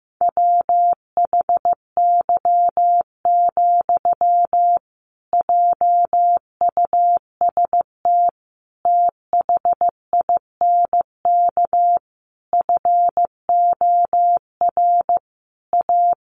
Morse code, heard as a keyed tone, about 15 wpm, 700 Hz. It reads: WHY, JUST THINK FOR A
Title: Sentences from Top 1000 Words 15wpm